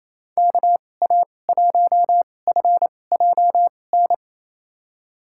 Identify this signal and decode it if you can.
Morse code, heard as a keyed tone, about 28 wpm, spaced as for 20 wpm, 700 Hz. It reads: KA1FJN
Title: USA Call Signs - 6 Characters - 2x Character Spacing - 28wpm